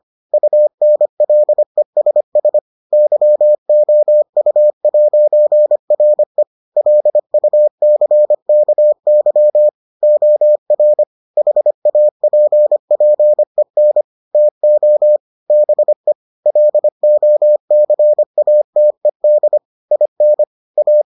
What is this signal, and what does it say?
Morse code, heard as a keyed tone, about 25 wpm, 600 Hz. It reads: UNLESS YOU'RE LUCKY OR HAPPEN TO BE LOCATED IN A